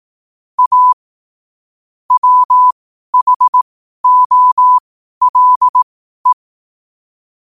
Morse code, heard as a keyed tone, about 18 wpm, spaced as for 12 wpm, 1000 Hz. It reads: A WHOLE
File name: qotdfiles / 12wpm